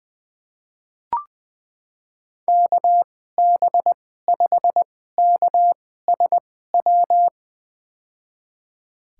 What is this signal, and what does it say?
Morse code, heard as a keyed tone, about 21 wpm, spaced as for 15 wpm, 700 Hz. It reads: KB5KSW
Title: USA Call Signs - 2x Character Spacing - 20wpm